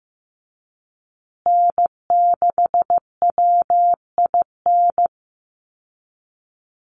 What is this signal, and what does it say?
Morse code, heard as a keyed tone, about 15 wpm, 700 Hz. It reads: N6WIN
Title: Call Signs Repeated 2x - 15wpm